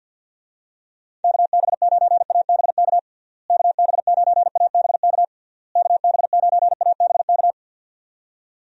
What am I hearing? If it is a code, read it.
Morse code, heard as a keyed tone, about 52 wpm, 700 Hz. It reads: KB9ABK KB9ABK KB9ABK
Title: USA Call Signs - 6 Characters - Repeated 3x - 50wpm